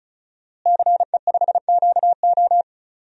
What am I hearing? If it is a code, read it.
Morse code, heard as a keyed tone, about 36 wpm, 700 Hz. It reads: CE5QO
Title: DXCC-100 Call Signs - 5 Characters - Repeated 2x - 35wpm